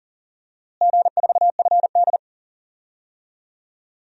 Morse code, heard as a keyed tone, about 40 wpm, 700 Hz. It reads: G4FD
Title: DXCC-100 Call Signs - 4 Characters - Repeated 3x - 40wpm